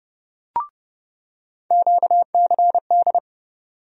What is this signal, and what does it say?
Morse code, heard as a keyed tone, about 29 wpm, 700 Hz. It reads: QCD
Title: USA Call Sign Postfixes - 3 Letters 30wpm